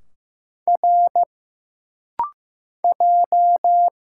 Morse code, heard as a keyed tone, about 15 wpm, 700 Hz. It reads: R J
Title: Single Letters - Rapid-Fire 15wpm